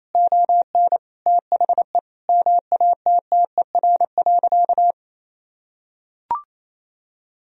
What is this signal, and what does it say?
Morse code, heard as a keyed tone, about 28 wpm, 700 Hz. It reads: ON THE MATTER.
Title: Sentences from Top 1000 Words - Review Part 1 - 28wpm